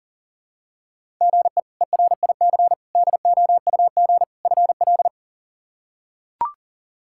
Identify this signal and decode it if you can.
Morse code, heard as a keyed tone, about 37 wpm, 700 Hz. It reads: GE ERIC DOUG FL
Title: K1USN SST QSO Practice - 40wpm